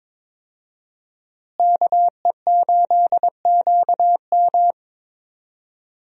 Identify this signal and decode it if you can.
Morse code, heard as a keyed tone, about 22 wpm, 700 Hz. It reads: KE8QM
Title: USA Call Signs - 5 Characters - Repeated 3x - 22wpm